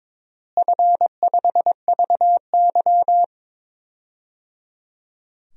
Morse code, heard as a keyed tone, about 22 wpm, 700 Hz. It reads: F5VY
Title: DXCC-100 Call Signs - 4 Characters - Repeated 3x - 22wpm